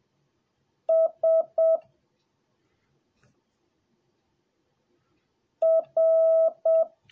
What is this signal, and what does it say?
Morse code, heard as a keyed tone, about 7 wpm, 650 Hz. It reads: S R